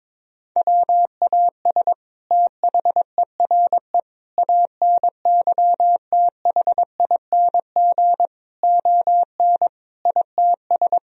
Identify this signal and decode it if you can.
Morse code, heard as a keyed tone, about 22 wpm, 700 Hz. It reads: WAS THERE ANYTHING ON ITS
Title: Sentences from Top 1000 Words - Review Part 3 - 22wpm